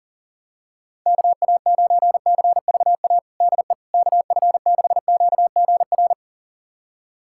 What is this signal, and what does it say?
Morse code, heard as a keyed tone, about 40 wpm, 700 Hz. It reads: KA9CVA DE KF6QGR
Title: USA Callsign DE Callsign 40wpm